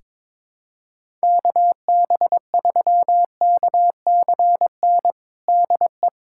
Morse code, heard as a keyed tone, about 21 wpm, 700 Hz. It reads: KB3KCN DE
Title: USA Callsign DE Callsign - 6 Characters - 22wpm